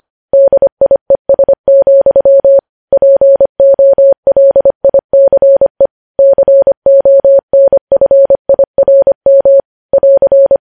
Morse code, heard as a keyed tone, about 25 wpm, 550 Hz. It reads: DIES, POLICE CONFIRM <AR>
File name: podcast / News headlines-1x-25x25